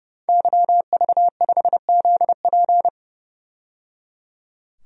Morse code, heard as a keyed tone, about 31 wpm, 700 Hz. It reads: YV5ZP
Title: DXCC-100 Call Signs - 5 Characters - 30wpm